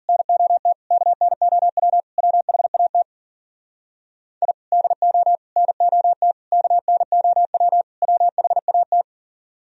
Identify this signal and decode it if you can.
Morse code, heard as a keyed tone, about 53 wpm, 700 Hz. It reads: NOT KNOW WHAT I DO NOT KNOW WHAT